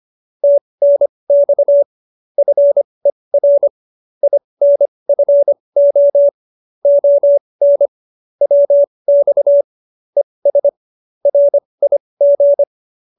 Morse code, heard as a keyed tone, about 25 wpm, spaced as for 20 wpm, 570 Hz. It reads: TNX FER INFO ON WX ES RIG